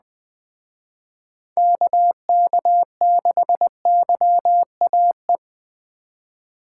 Morse code, heard as a keyed tone, about 20 wpm, 700 Hz. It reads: KK6YAE
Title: Speed Racing - USA 2x3 Call Signs 17wpm